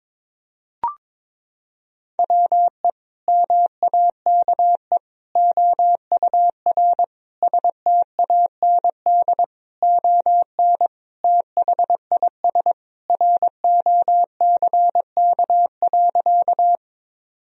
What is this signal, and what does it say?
Morse code, heard as a keyed tone, about 22 wpm, 700 Hz. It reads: WE MAKE OUR STAND ON THIS ROCK.
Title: Sentences from Top 1000 Words - Review Part 2 - 22wpm